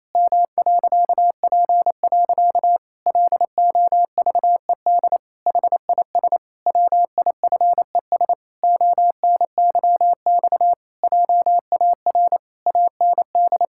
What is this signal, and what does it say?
Morse code, heard as a keyed tone, about 28 wpm, 700 Hz. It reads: M.P. LOVED HIS WIFES ONYX JAR AND
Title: Pangram Challenge v2 - 28wpm